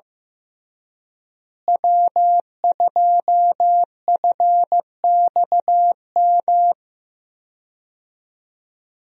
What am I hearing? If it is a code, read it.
Morse code, heard as a keyed tone, about 15 wpm, 700 Hz. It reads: W2FXM
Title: USA Call Signs - 5 Characters - Repeated 2x - 15wpm